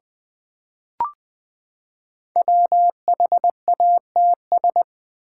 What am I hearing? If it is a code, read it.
Morse code, heard as a keyed tone, about 20 wpm, 700 Hz. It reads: WHATS
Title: Build A Word - 5 Letters At A Time - 10-Letter Words - 20wpm